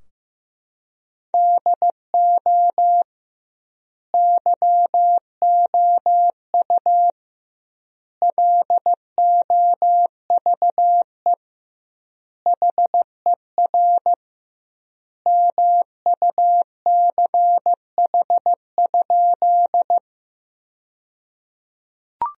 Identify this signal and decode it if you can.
Morse code, heard as a keyed tone, about 16 wpm, 700 Hz. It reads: DO YOU LOVE HER MUCH?
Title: 2x Word-Spacing - Sentences from Top 400 Words - 15wpm